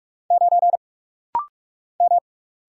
Morse code, heard as a keyed tone, about 46 wpm, 700 Hz. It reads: 9 M
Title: Single Letter-Number Alt - Rapid-Fire 45wpm